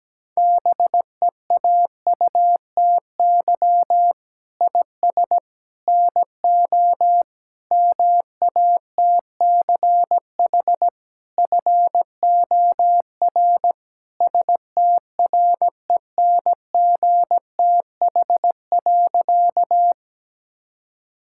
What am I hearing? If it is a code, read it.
Morse code, heard as a keyed tone, about 17 wpm, 700 Hz. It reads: BEAUTY IS NO MATCH FOR STRENGTH.
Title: Sentences from Top 1600 Words 17wpm